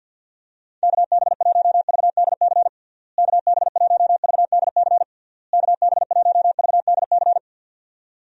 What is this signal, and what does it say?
Morse code, heard as a keyed tone, about 50 wpm, 700 Hz. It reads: KB1VDC KB1VDC KB1VDC
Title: USA Call Signs - 6 Characters - Repeated 3x - 50wpm